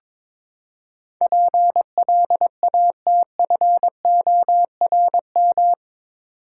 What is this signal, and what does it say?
Morse code, heard as a keyed tone, about 22 wpm, 700 Hz. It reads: PLATFORM
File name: practice / 22wpm / Eight Letter Words - 22wpm